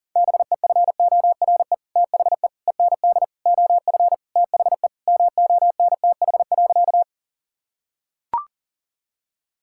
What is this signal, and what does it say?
Morse code, heard as a keyed tone, about 37 wpm, 700 Hz. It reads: BEFORE THE END OF THE MONTH.
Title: Sentences from Top 1000 Words - Review Part 5 - 40wpm